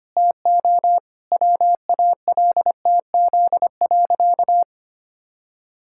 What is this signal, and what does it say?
Morse code, heard as a keyed tone, about 25 wpm, 700 Hz. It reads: TO WALTZ.
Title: Pangram Challenge v2 - 25wpm